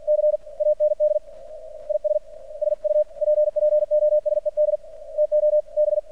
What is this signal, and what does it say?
Morse code, heard as a keyed tone, about 48 wpm, 600 Hz, with radio noise. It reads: C AND IS SUPPOSED TO B
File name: archived / CRSnet